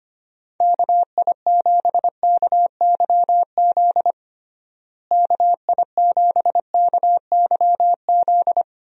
Morse code, heard as a keyed tone, about 25 wpm, 700 Hz. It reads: KI7KYZ KI7KYZ
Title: USA Call Signs - 6 Characters - Repeated 2x - 25wpm